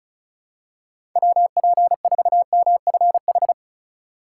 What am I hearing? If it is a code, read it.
Morse code, heard as a keyed tone, about 36 wpm, 700 Hz. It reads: WP4MFH
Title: USA Call Signs - 3x Character Spacing - 35wpm